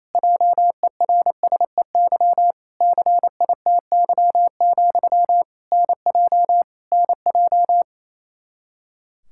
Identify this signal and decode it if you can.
Morse code, heard as a keyed tone, about 28 wpm, 700 Hz. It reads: JERSEY CITY, NJ NJ
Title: Practice QSO - Exchange 1 - Beginning - 28wpm